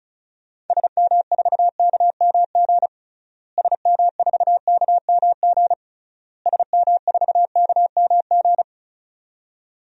Morse code, heard as a keyed tone, about 34 wpm, 700 Hz. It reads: SM4KMG SM4KMG SM4KMG
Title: DXCC-100 Call Signs - 6 Characters - Repeated 3x - 35wpm